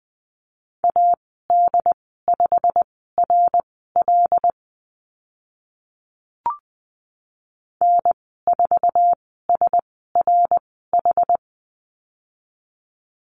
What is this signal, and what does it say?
Morse code, heard as a keyed tone, about 20 wpm, spaced as for 14 wpm, 700 Hz. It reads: AD5RL N4SRH
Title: 088 Lesson 15g - Number 5 - Call Signs 20wpm